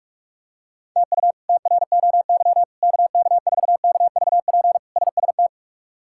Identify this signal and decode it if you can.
Morse code, heard as a keyed tone, about 46 wpm, 700 Hz. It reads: TU TROY KK4KVP SST
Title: K1USN SST QSO Practice - 45wpm